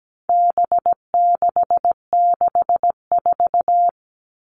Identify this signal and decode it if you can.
Morse code, heard as a keyed tone, about 17 wpm, 700 Hz. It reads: B664
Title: All Cut Numbers - 4-Digit 17wpm